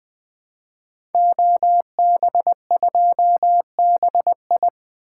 Morse code, heard as a keyed tone, about 20 wpm, 700 Hz. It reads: OB2BI
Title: Code Groups - 5 Letter:Numbers - 20wpm